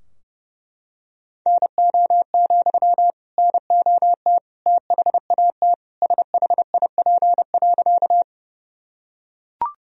Morse code, heard as a keyed tone, about 30 wpm, 700 Hz. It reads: NO, NOT THAT SHIP.
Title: Sentences from Top 1000 Words - Review Part 2 - 30wpm